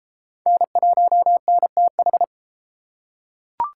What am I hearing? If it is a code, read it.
Morse code, heard as a keyed tone, about 33 wpm, 700 Hz. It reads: N1NTH